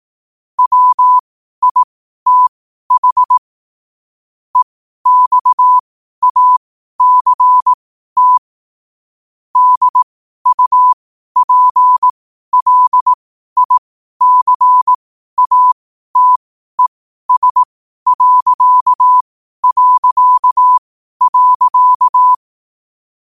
Morse code, heard as a keyed tone, about 18 wpm, spaced as for 12 wpm, 1000 Hz. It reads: WITH EXACT DUPLICATES...
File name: qotdfiles / 12wpm